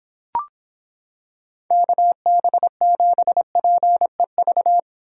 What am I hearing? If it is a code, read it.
Morse code, heard as a keyed tone, about 26 wpm, 700 Hz. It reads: KB7PEV